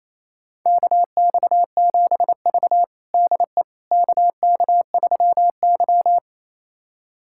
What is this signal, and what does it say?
Morse code, heard as a keyed tone, about 27 wpm, 700 Hz. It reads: KX7V DE KK3Y
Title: USA Callsign DE Callsign - 4 Characters - 28wpm